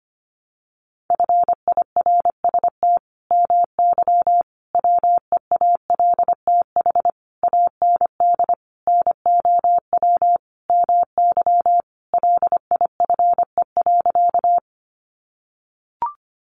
Morse code, heard as a keyed tone, about 25 wpm, 700 Hz. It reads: FIRST MY WEALTH AND NOW MY LIFE.
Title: Sentences from Top 1200 Words 25wpm